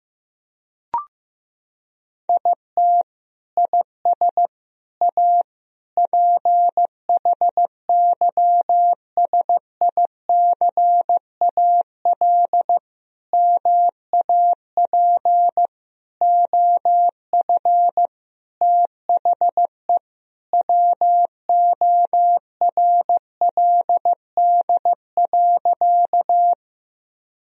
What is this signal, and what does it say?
Morse code, heard as a keyed tone, about 15 wpm, 700 Hz. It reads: IT IS A PHYSICAL MAP OF THE WORLD.
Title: Sentences from Top 1000 Words - Review Part 4 - 15wpm